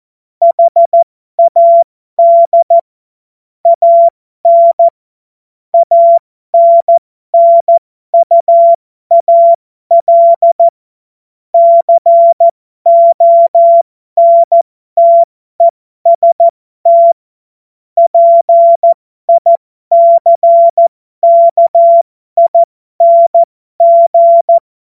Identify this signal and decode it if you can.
Morse code, heard as a keyed tone, about 14 wpm, spaced as for 12 wpm, 680 Hz. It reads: HAD AN ANNUAL CONTEST PICKING
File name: Morsekurs / Witze